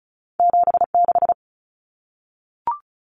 Morse code, heard as a keyed tone, about 36 wpm, 700 Hz. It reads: 76
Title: Numbers - 2-Digit 35wpm